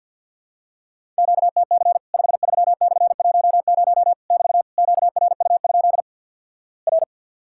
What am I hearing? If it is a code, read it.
Morse code, heard as a keyed tone, about 48 wpm, 700 Hz. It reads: QTC 53/10 <BT> QRU? R
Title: Worked All Europe DX Contest - QTC Exchange with Repeats - 50wpm